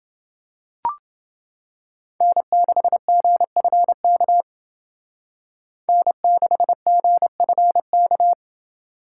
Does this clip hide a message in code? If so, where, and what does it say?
Yes: Morse code from 2.2 to 8.3 s, heard as a keyed tone, about 28 wpm, 700 Hz. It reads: N6GFK N6GFK